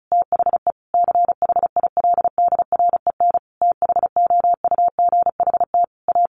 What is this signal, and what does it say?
Morse code, heard as a keyed tone, about 34 wpm, 700 Hz. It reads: THE CHILDREN THOUGHT A
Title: Sentences from Top 800 Words 35wpm